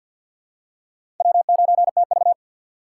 Morse code, heard as a keyed tone, about 48 wpm, 700 Hz. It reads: W9TV
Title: USA Call Signs - 4 Characters - 3x Character Spacing - 50wpm